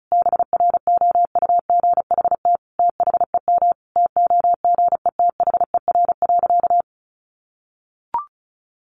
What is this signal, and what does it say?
Morse code, heard as a keyed tone, about 34 wpm, 700 Hz. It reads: BROUGHT THEM TOGETHER.
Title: Sentences from Top 1000 Words 35wpm